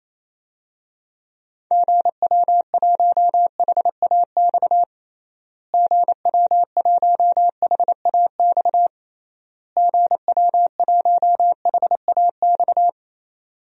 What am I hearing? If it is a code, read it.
Morse code, heard as a keyed tone, about 28 wpm, 700 Hz. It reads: GW1HAX GW1HAX GW1HAX
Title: DXCC-100 Call Signs - 6 Characters - Repeated 3x - 28wpm